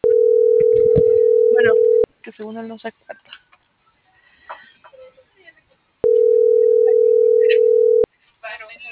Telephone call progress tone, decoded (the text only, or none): ringback tone